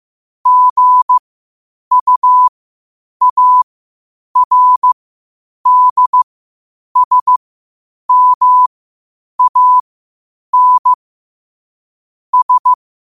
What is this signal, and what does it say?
Morse code, heard as a keyed tone, about 15 wpm, spaced as for 8 wpm, 1000 Hz. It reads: GUARDSMAN S
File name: qotdfiles / 8wpm